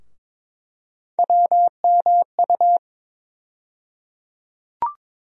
Morse code, heard as a keyed tone, about 22 wpm, 700 Hz. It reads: WMU
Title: USA Call Sign Postfixes - 3 Letters 22wpm